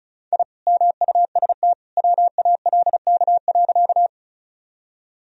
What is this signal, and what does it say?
Morse code, heard as a keyed tone, about 34 wpm, 700 Hz. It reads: I MUST WALK.
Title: Sentences from Top 1000 Words - Review Part 1 - 35wpm